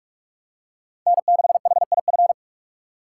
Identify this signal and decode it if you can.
Morse code, heard as a keyed tone, about 42 wpm, 700 Hz. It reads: N6HIF
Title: USA Call Signs - 3x Character Spacing - 45wpm